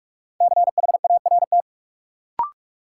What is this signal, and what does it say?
Morse code, heard as a keyed tone, about 46 wpm, 700 Hz. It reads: CHART
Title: Sentences from Top 1300 Words - New Word Drill - 45wpm